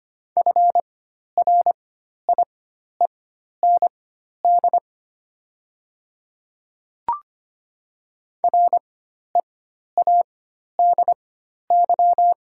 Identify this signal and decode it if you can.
Morse code, heard as a keyed tone, about 25 wpm, spaced as for 12 wpm, 690 Hz. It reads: FRIEND READY
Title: 4x Character Spacing - Top 500 Words - 25wpm